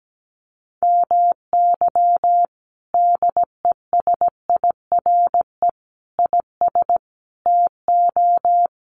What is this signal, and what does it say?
Morse code, heard as a keyed tone, about 17 wpm, 700 Hz. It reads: MY DESIRE IS TO
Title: Sentences from Top 1100 Words 17wpm